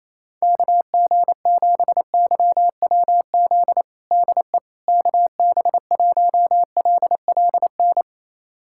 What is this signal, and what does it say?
Morse code, heard as a keyed tone, about 28 wpm, 700 Hz. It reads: KG7YWZ DE KB1LLN
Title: USA Callsign DE Callsign - 6 Characters - 28wpm